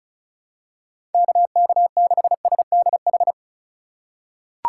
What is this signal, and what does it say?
Morse code, heard as a keyed tone, about 36 wpm, 700 Hz. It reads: KK6SDH
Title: USA Call Signs - 4x Character Spacing - 35wpm